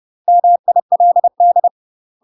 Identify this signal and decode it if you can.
Morse code, heard as a keyed tone, about 30 wpm, 700 Hz. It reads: MILD
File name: UK MASTER QSOs TRAINING / 30WPM / UK MASTER QSOs 0005